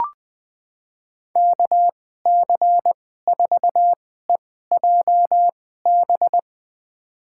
Morse code, heard as a keyed tone, about 20 wpm, spaced as for 14 wpm, 700 Hz. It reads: KC4EJB
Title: USA Call Signs - 6 Characters - 2x Character Spacing - 20wpm